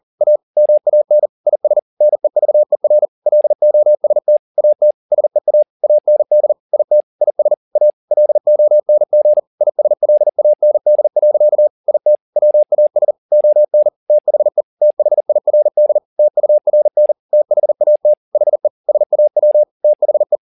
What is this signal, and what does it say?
Morse code, heard as a keyed tone, about 40 wpm, 600 Hz. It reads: A MAN IS NEVER LOST AT SEA AND IT IS A LONG ISLAND. IT WAS ON THE THIRD TURN THAT HE SAW THE
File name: the old man and the sea 2 40wpm